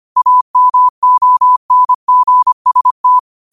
Morse code, heard as a keyed tone, about 25 wpm, 1000 Hz. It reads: AMONGST
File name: qotdfiles / 25wpm